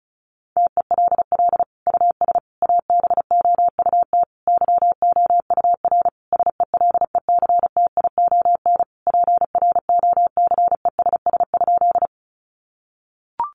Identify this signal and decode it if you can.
Morse code, heard as a keyed tone, about 34 wpm, 700 Hz. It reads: TELL US ABOUT YOUR SELECTION PROCESS?
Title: Sentences from Top 1400 Words 35wpm